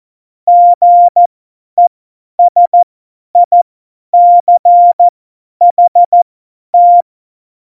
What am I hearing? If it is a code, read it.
Morse code, heard as a keyed tone, about 14 wpm, spaced as for 10 wpm, 700 Hz. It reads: GESICHT
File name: Morsekurs / Witze / Witz 29. September 2020, 50 F70